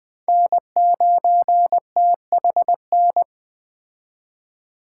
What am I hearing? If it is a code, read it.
Morse code, heard as a keyed tone, about 20 wpm, 700 Hz. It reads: N9THN
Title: Speed Racing - USA 1x3 Call Signs 17wpm